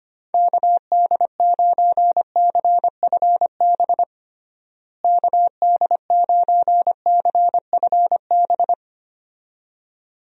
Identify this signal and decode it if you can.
Morse code, heard as a keyed tone, about 25 wpm, 700 Hz. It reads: KD9CFB KD9CFB
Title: USA Call Signs - 6 Characters - Repeated 2x - 25wpm